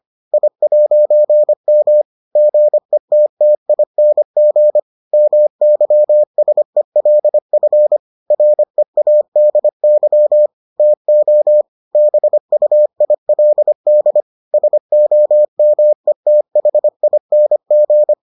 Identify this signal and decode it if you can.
Morse code, heard as a keyed tone, about 25 wpm, 600 Hz. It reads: I'M GETTING MYSELF READY TO BUILD SOMETHING